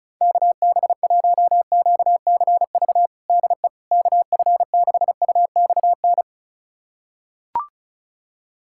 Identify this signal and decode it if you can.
Morse code, heard as a keyed tone, about 36 wpm, 700 Hz. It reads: KB1QCV DE KF6UXN
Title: USA Callsign DE Callsign - 6 Characters - 35wpm